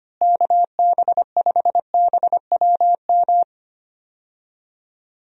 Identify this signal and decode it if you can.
Morse code, heard as a keyed tone, about 25 wpm, 700 Hz. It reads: KB5BWM
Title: USA Call Signs - 6 Characters - Repeated 2x - 25wpm